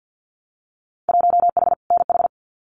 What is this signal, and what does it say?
Morse code, heard as a keyed tone, about 50 wpm, 700 Hz. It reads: 1H NH
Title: Winter Field Day Exchanges 50wpm